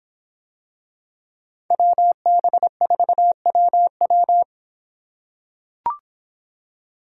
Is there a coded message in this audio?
Yes: Morse code from 1.7 to 4.4 s, heard as a keyed tone, about 25 wpm, 700 Hz. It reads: WB4WW